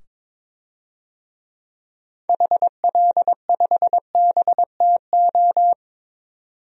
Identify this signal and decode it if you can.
Morse code, heard as a keyed tone, about 21 wpm, 700 Hz. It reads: HL5BTO
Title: DXCC-100 Call Signs - 6 Characters - 22wpm